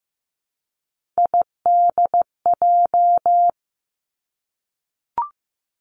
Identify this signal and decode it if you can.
Morse code, heard as a keyed tone, about 15 wpm, 700 Hz. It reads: IDJ
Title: Three Challenging Letter-Number Combinations 15wpm